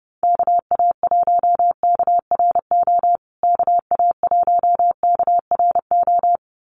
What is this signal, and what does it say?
Morse code, heard as a keyed tone, about 30 wpm, 700 Hz. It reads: KA1KRO KA1KRO
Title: USA Call Signs Repeated 2x 30wpm